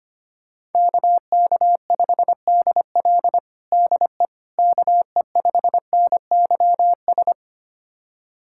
Morse code, heard as a keyed tone, about 25 wpm, 700 Hz. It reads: KK5DL DE KE5NYS
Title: USA Callsign DE Callsign 25wpm